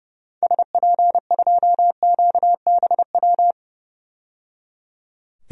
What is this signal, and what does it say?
Morse code, heard as a keyed tone, about 29 wpm, 700 Hz. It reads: SP2QBW
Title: DXCC-100 Call Signs - 6 Characters - 30wpm